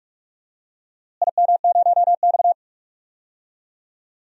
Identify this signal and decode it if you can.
Morse code, heard as a keyed tone, about 46 wpm, 700 Hz. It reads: IM0X
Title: DXCC-100 Call Signs - 4 Characters - Repeated 2x - 45wpm